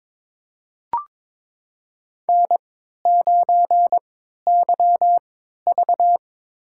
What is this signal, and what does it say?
Morse code, heard as a keyed tone, about 22 wpm, spaced as for 12 wpm, 700 Hz. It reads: N9YV
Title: USA Call Signs - 3x Character Spacing - 22wpm